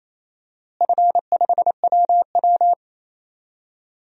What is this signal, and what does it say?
Morse code, heard as a keyed tone, about 28 wpm, 700 Hz. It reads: F5WW
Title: DXCC-100 Call Signs - 4 Characters - Repeated 3x - 28wpm